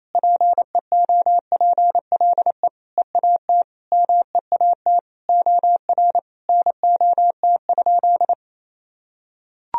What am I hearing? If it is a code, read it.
Morse code, heard as a keyed tone, about 28 wpm, 700 Hz. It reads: PEOPLE EAT MEAT OR NOT?
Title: Sentences from Top 1000 Words - Review Part 3 - 28wpm